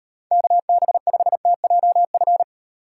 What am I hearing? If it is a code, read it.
Morse code, heard as a keyed tone, about 38 wpm, 700 Hz. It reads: KB5TJF